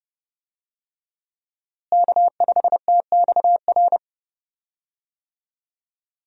Morse code, heard as a keyed tone, about 31 wpm, 700 Hz. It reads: K5TXR
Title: USA Call Signs - 5 Characters - Repeated 2x - 30wpm